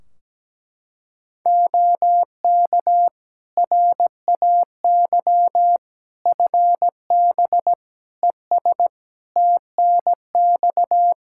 Morse code, heard as a keyed tone, about 17 wpm, 700 Hz. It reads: OK RAY FB ES TNX
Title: Practice QSO - Exchange 2 - Info - 17wpm